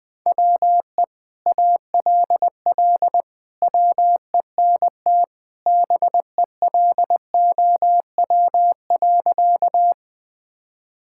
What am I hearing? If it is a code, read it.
Morse code, heard as a keyed tone, about 20 wpm, 700 Hz. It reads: WE ALL WENT BELOW.
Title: Sentences from Top 1000 Words - Review Part 5 - 20wpm